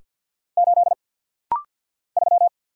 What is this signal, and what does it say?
Morse code, heard as a keyed tone, about 49 wpm, 700 Hz. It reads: 8 3
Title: Single Letter-Number - Rapid-Fire 50wpm